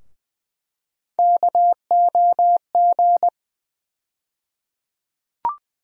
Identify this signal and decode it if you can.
Morse code, heard as a keyed tone, about 21 wpm, 700 Hz. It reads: KOG
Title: USA Call Sign Postfixes - 3 Letters 20wpm